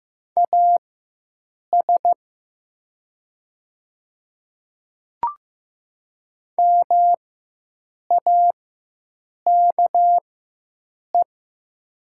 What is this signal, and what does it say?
Morse code, heard as a keyed tone, about 15 wpm, spaced as for 7 wpm, 700 Hz. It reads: AS MAKE
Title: 4x Character Spacing - Top 100 Words - 15wpm